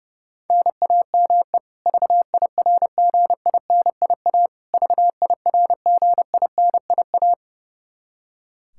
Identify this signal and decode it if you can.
Morse code, heard as a keyed tone, about 30 wpm, 700 Hz. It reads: NAME VIRGINIA VIRGINIA
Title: Practice QSO - Exchange 1 - Beginning - 30wpm